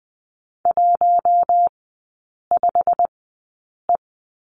Morse code, heard as a keyed tone, about 20 wpm, spaced as for 8 wpm, 690 Hz. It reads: 15E
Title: 145 Lesson 24e - Letter y - Three Characters 20wpm